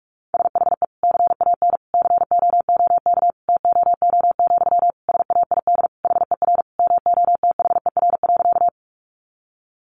Morse code, heard as a keyed tone, about 44 wpm, 700 Hz. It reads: SHE CAN COOK TOO, SAID HER MOTHER.
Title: Sentences from Top 700 Words 45wpm